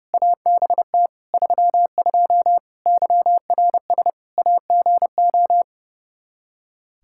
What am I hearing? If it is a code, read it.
Morse code, heard as a keyed tone, about 30 wpm, 700 Hz. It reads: ABT 32 YRS AGO
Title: Practice QSO - Exchange 3 - More Info - 30wpm